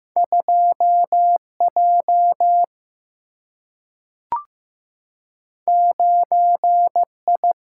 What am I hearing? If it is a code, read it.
Morse code, heard as a keyed tone, about 15 wpm, 700 Hz. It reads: 2J 9I
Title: Single-Letter Single-Number 15wpm